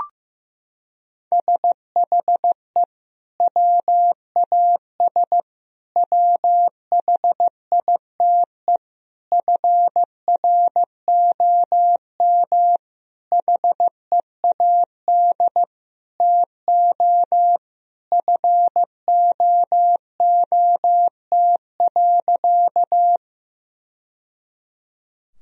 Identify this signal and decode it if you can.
Morse code, heard as a keyed tone, about 15 wpm, 700 Hz. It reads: SHE WAS WHITE FROM HEAD TO FOOT.
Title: Sentences from Top 1000 Words - Review Part 3 - 15wpm